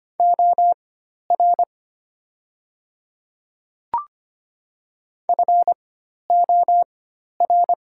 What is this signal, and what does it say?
Morse code, heard as a keyed tone, about 25 wpm, spaced as for 12 wpm, 700 Hz. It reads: OR FOR
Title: 4x Character Spacing - Top 500 Words - 25wpm